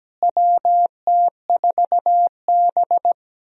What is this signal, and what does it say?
Morse code, heard as a keyed tone, about 17 wpm, 700 Hz. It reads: WT4B
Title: USA Call Signs - 4 Characters - Repeated 3x - 17wpm